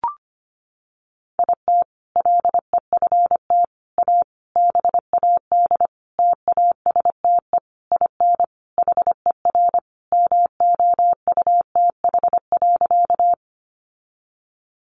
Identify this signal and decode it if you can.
Morse code, heard as a keyed tone, about 25 wpm, 700 Hz. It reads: IT LEFT A BAD TASTE IN HER MOUTH.
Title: Sentences from Top 1500 Words 25wpm